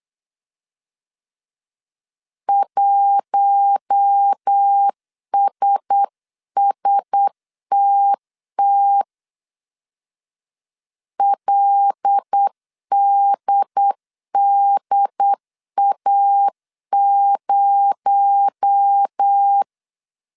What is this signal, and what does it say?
Morse code, heard as a keyed tone, about 8 wpm, 800 Hz. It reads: 1SSTT LDDA0
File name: 8wpm